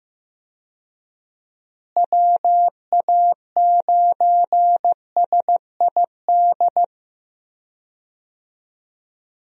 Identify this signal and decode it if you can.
Morse code, heard as a keyed tone, about 15 wpm, 700 Hz. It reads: WA9SID
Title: USA Call Signs - 6 Characters - Repeated 2x - 15wpm